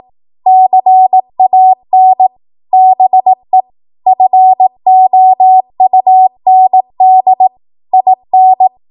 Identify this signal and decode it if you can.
Morse code, heard as a keyed tone, about 18 wpm, 750 Hz. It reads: CAN BE FOUND IN
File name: Morse / 18WPM